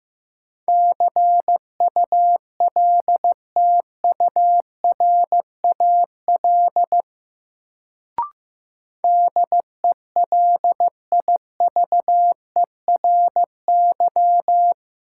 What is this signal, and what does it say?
Morse code, heard as a keyed tone, about 15 wpm, 700 Hz. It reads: CULTURAL DELIVERY
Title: Sentences from Top 1500 Words - New Word Drill Alt - 15wpm